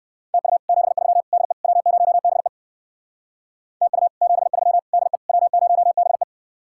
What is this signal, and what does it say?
Morse code, heard as a keyed tone, about 56 wpm, 700 Hz. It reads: TU 73 DE W0BE TU 73 DE W0BE
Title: Speed-Racing Common QSO Phrases Encore - 45wpm